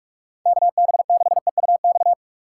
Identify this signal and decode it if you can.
Morse code, heard as a keyed tone, about 45 wpm, 700 Hz. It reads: KB6EUX